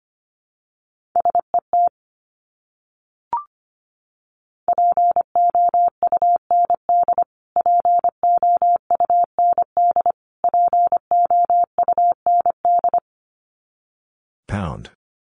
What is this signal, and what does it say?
Morse code, heard as a keyed tone, about 25 wpm, 700 Hz. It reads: SET POUND POUND POUND
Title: Top 500 Words Repeated 25wpm